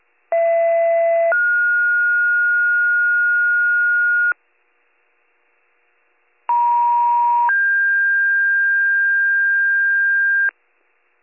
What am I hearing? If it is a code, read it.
Two-tone page, heard as a one-second tone, then a three-second tone, about 670 then 1430 Hz, about 950 then 1600 Hz. It is heard from 0.3 to 10.5 s.